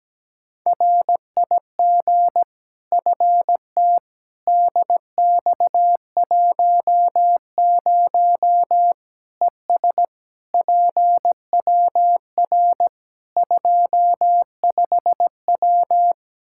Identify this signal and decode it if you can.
Morse code, heard as a keyed tone, about 17 wpm, 700 Hz. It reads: RIG FT DX10 ES PWR 25W
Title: Practice QSO - Exchange 2 - Info - 17wpm